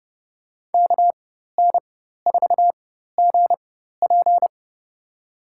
Morse code, heard as a keyed tone, about 30 wpm, spaced as for 14 wpm, 700 Hz. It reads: KN4GP
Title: USA Call Signs - 4x Character Spacing - 30wpm